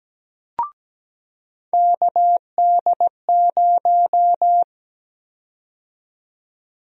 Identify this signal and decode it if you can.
Morse code, heard as a keyed tone, about 17 wpm, 700 Hz. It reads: KD0
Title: USA Call Sign Prefixes - 17wpm